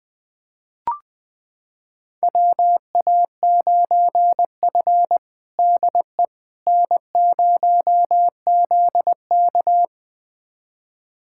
Morse code, heard as a keyed tone, about 20 wpm, 700 Hz. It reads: WA9F DE N0ZK
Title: USA Callsign DE Callsign - 4 Characters - 20wpm